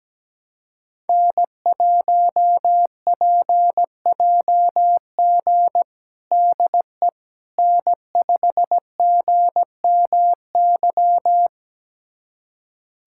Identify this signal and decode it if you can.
Morse code, heard as a keyed tone, about 17 wpm, 700 Hz. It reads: N1PJG DE N5GMY
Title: USA Callsign DE Callsign - 5 Characters - 17wpm